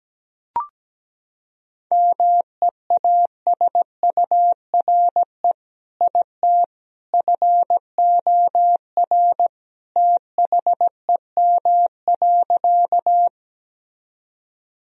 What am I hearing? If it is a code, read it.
Morse code, heard as a keyed tone, about 17 wpm, 700 Hz. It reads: MEASURE IT FOR THEM.
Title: Sentences from Top 1000 Words - Review Part 2 - 17wpm